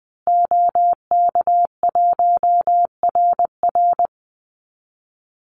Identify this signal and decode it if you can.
Morse code, heard as a keyed tone, about 20 wpm, 700 Hz. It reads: OK1RR
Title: Call Signs Repeated 2x - 20wpm